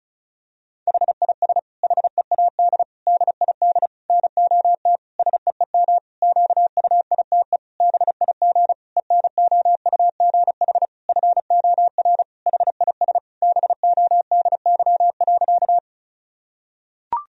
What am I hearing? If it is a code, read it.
Morse code, heard as a keyed tone, about 34 wpm, 700 Hz. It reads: HIS HEAD DID NOT SEEM QUITE BIG ENOUGH FOR HIS BODY.
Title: Sentences from Top 1000 Words - Review Part 2 - 35wpm